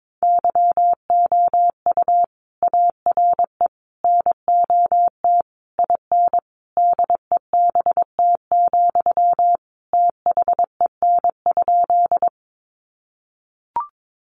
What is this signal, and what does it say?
Morse code, heard as a keyed tone, about 22 wpm, 700 Hz. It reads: YOU ARE NOT IN DEBT, THEN?
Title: Sentences from Top 1300 Words 22wpm